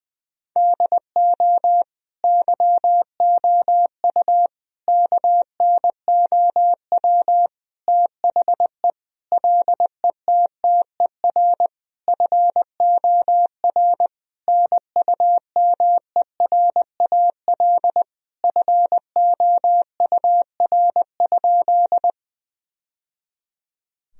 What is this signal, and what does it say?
Morse code, heard as a keyed tone, about 21 wpm, 700 Hz. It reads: DO YOU KNOW THE LETTER FOR NUMERAL FOUR?
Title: Sentences from Top 1000 Words - Review Part 2 - 20wpm